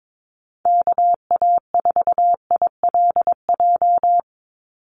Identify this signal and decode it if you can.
Morse code, heard as a keyed tone, about 22 wpm, 700 Hz. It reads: KA4ILJ
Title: USA Call Signs 22wpm